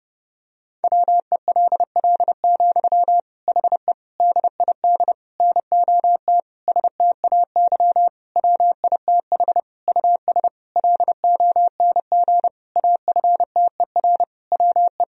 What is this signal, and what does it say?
Morse code, heard as a keyed tone, about 30 wpm, 700 Hz. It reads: WELL, HE DID NOT STAY WITH US LONG AFTER WE
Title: Sentences from Top 1000 Words - Review Part 2 - 30wpm